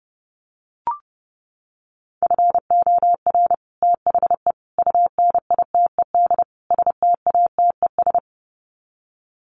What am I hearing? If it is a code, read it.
Morse code, heard as a keyed tone, about 30 wpm, 700 Hz. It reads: FOR THE UNITED STATES
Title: Sets of 4 Words 30wpm